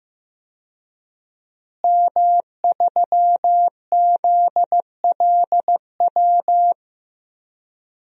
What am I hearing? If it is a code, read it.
Morse code, heard as a keyed tone, about 15 wpm, 700 Hz. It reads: M3ZLW
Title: DXCC-100 Call Signs - 5 Characters - Repeated 2x - 15wpm